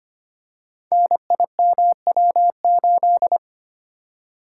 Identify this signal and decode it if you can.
Morse code, heard as a keyed tone, about 25 wpm, 700 Hz. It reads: NIMW8
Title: Code Groups - 5 Letter:Numbers - 25wpm